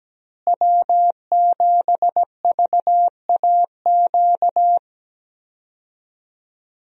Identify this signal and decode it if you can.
Morse code, heard as a keyed tone, about 17 wpm, 700 Hz. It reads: W7VAQ
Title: USA Call Signs - 5 Characters - Repeated 2x - 17wpm